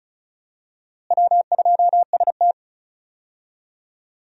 Morse code, heard as a keyed tone, about 36 wpm, 700 Hz. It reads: W2ST
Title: USA Call Signs - 4 Characters - Repeated 2x - 35wpm